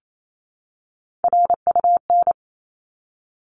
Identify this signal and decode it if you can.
Morse code, heard as a keyed tone, about 28 wpm, 700 Hz. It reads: RUN
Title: Top 300 Words 28wpm